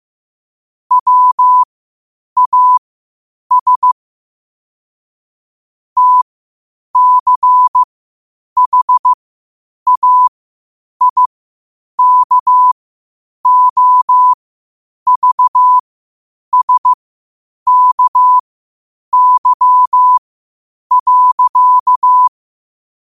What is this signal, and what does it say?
Morse code, heard as a keyed tone, about 15 wpm, spaced as for 8 wpm, 1000 Hz. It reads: WAS TCHAIKOVSKY.
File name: qotdfiles / 8wpm